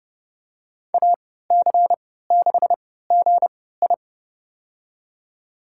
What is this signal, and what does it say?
Morse code, heard as a keyed tone, about 32 wpm, spaced as for 17 wpm, 700 Hz. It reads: AC6GI
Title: USA Call Signs - 5 Characters - 3x Character Spacing - 30wpm